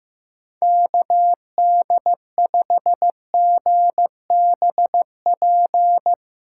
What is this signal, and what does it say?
Morse code, heard as a keyed tone, about 15 wpm, 700 Hz. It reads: KD5GBP
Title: Speed Racing - USA 2x3 Call Signs 15wpm